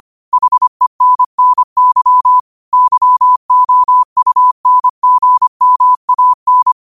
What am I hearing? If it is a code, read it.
Morse code, heard as a keyed tone, about 25 wpm, 1000 Hz. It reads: HENNY YOUNGMAN
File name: 25wpm